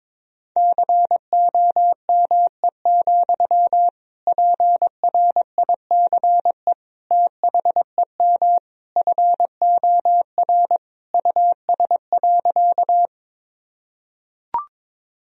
Morse code, heard as a keyed tone, about 22 wpm, 700 Hz. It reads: COME, PRICE THEM FOR US.
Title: Sentences from Top 1000 Words - Review Part 4 - 22wpm